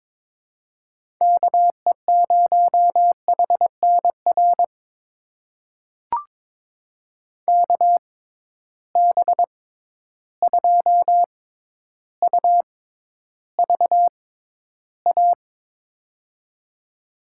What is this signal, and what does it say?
Morse code, heard as a keyed tone, about 22 wpm, spaced as for 11 wpm, 700 Hz. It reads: KE0HNR KB2UVA
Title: USA Call Signs - 6 Characters - 6x Character Spacing - 22wpm